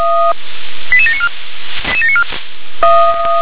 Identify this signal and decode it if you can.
Five-tone selective call (ZVEI), heard as quick runs of five tones, tones of about 70 ms. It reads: 88794